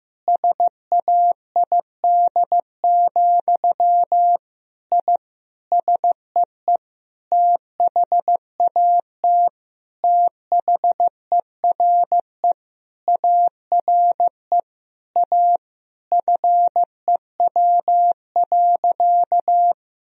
Morse code, heard as a keyed tone, about 15 wpm, 700 Hz. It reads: SAID, I SEE THAT THERE ARE A FEW.
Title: Sentences from Top 1000 Words - Review Part 1 - 15wpm